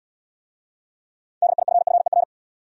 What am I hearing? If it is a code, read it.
Morse code, heard as a keyed tone, about 71 wpm, 700 Hz. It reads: DIFFER